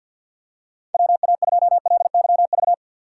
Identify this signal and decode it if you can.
Morse code, heard as a keyed tone, about 52 wpm, 700 Hz. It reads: WA2LYV
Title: USA Call Signs - 3x Character Spacing - 50wpm